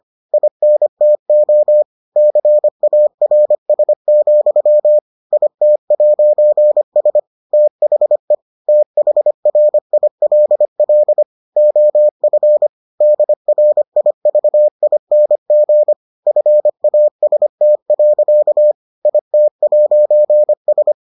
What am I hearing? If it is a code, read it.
Morse code, heard as a keyed tone, about 25 wpm, 600 Hz. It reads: INTO CARS, IT'S THE THRILL OF DRIVING FAST. IT'S